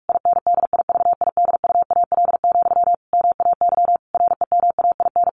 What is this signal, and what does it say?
Morse code, heard as a keyed tone, about 44 wpm, 700 Hz. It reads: INDIVIDUAL, MAY REMAIN
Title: Sentences from Top 1400 Words 45wpm